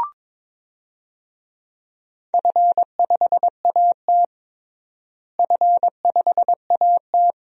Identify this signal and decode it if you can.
Morse code, heard as a keyed tone, about 22 wpm, 700 Hz. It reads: F5AT F5AT
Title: DXCC-100 Call Signs - 4 Characters - Repeated 2x - 22wpm